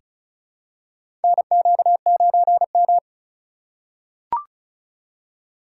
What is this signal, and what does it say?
Morse code, heard as a keyed tone, about 34 wpm, 700 Hz. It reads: NQ9M
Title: USA Call Signs - 2x Character Spacing - 35wpm